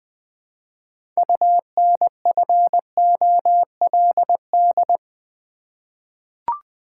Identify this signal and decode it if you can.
Morse code, heard as a keyed tone, about 20 wpm, 700 Hz. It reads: UNFOLD
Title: 6 Letter Words - First 3 Letters Given - 20wpm